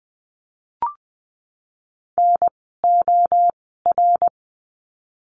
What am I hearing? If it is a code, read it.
Morse code, heard as a keyed tone, about 20 wpm, spaced as for 15 wpm, 700 Hz. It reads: NOR
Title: 054 Lesson 10f - Letter r - Words 20wpm